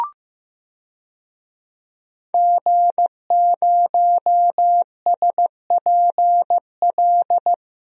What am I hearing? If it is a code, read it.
Morse code, heard as a keyed tone, about 15 wpm, 700 Hz. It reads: G0SPL
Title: DXCC-100 Call Signs - 5 Characters - Repeated 3x - 15wpm